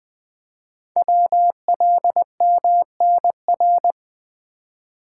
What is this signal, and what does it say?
Morse code, heard as a keyed tone, about 20 wpm, 700 Hz. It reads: WLMNR
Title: Code Groups - 5 Letters - 20wpm